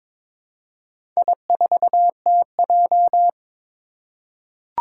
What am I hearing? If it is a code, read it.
Morse code, heard as a keyed tone, about 22 wpm, 700 Hz. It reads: I4TJ
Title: Four Challenging Letter-Number Combinations 22wpm